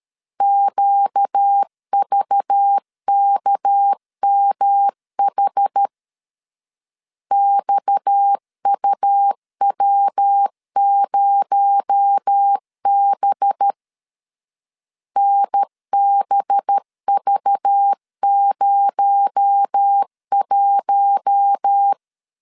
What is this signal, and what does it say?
Morse code, heard as a keyed tone, about 13 wpm, 800 Hz. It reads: QVKMH XUW0B NBV01
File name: LEZIONI / 6 PKX0 / 6 PKX0 12wpm 2231